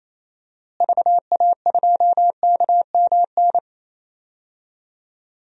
Morse code, heard as a keyed tone, about 27 wpm, 700 Hz. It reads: VA2KMN
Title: DXCC-100 Call Signs - 6 Characters - Repeated 3x - 28wpm